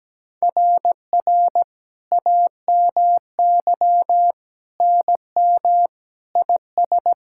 Morse code, heard as a keyed tone, about 17 wpm, 700 Hz. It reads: RR AMY NM IS
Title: SOTA QSO Practice 17wpm